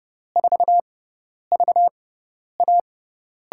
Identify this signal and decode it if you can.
Morse code, heard as a keyed tone, about 32 wpm, spaced as for 10 wpm, 700 Hz. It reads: 4VA
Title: USA Call Signs - 6 Characters - 6x Character Spacing - 30wpm